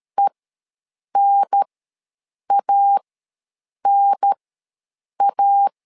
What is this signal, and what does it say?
Morse code, heard as a keyed tone, about 13 wpm, spaced as for 7 wpm, 800 Hz. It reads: ENANA
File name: LEZIONI / 2 UHDN / 2 UHDN ws 12wpm 1708